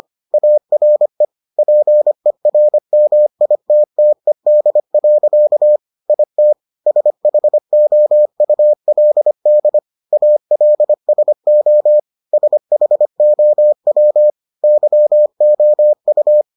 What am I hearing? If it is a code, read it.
Morse code, heard as a keyed tone, about 25 wpm, 600 Hz. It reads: ARE PERMITTED. IT SHOULD ALSO SHOW YOU